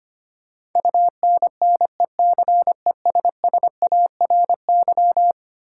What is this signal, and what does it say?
Morse code, heard as a keyed tone, about 25 wpm, 700 Hz. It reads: UNNECESSARY
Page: morse-code-ninja-practice-25wpm